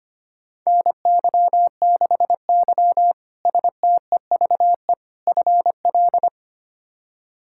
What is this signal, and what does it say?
Morse code, heard as a keyed tone, about 25 wpm, 700 Hz. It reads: NY6Y STEVE FL
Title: K1USN SST QSO Practice - 25wpm